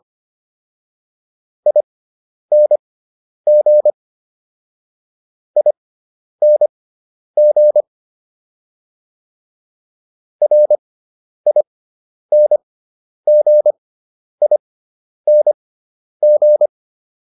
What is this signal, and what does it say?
Morse code, heard as a keyed tone, about 25 wpm, spaced as for 10 wpm, 600 Hz. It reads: ING ING RINGING